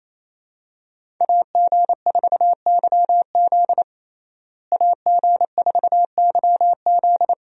Morse code, heard as a keyed tone, about 28 wpm, 700 Hz. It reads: AG4YZ AG4YZ
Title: USA Call Signs - 5 Characters - Repeated 2x - 28wpm